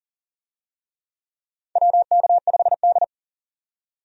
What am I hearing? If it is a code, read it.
Morse code, heard as a keyed tone, about 40 wpm, 700 Hz. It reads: WK5D
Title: USA Call Signs - 4 Characters - 40wpm